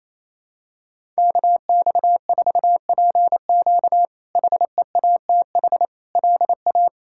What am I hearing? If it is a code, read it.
Morse code, heard as a keyed tone, about 28 wpm, 700 Hz. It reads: KX4PQ HEATH LA
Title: K1USN SST QSO Practice - 28wpm